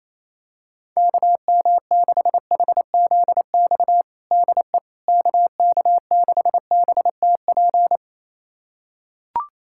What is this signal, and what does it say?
Morse code, heard as a keyed tone, about 28 wpm, 700 Hz. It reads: KM6HZX DE KK6BTP
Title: USA Callsign DE Callsign 28wpm